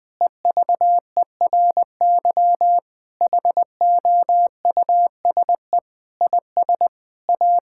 Morse code, heard as a keyed tone, about 20 wpm, 700 Hz. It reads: EVERY HOUSE IS A
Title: Sentences from Top 1600 Words 20wpm